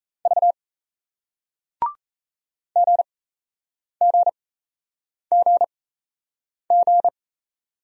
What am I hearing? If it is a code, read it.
Morse code, heard as a keyed tone, about 38 wpm, spaced as for 8 wpm, 700 Hz. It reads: V GGGG